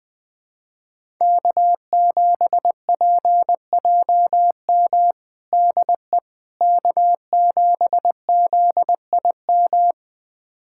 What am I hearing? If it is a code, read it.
Morse code, heard as a keyed tone, about 21 wpm, 700 Hz. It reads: K7PJM DE K7ZIM
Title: USA Callsign DE Callsign - 5 Characters - 20wpm